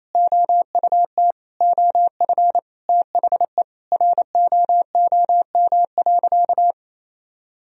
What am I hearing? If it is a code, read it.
Morse code, heard as a keyed tone, about 27 wpm, 700 Hz. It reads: OUT OF THE ROOM.
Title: Sentences from Top 500 Words and Abbreviations 28wpm